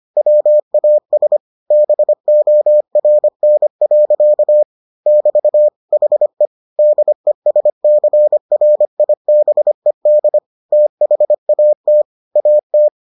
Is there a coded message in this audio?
Yes: Morse code from 0.2 to 12.9 s, heard as a keyed tone, about 25 wpm, 600 Hz. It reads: WAS BORN. <BT> HE DESCRIBED THAT AT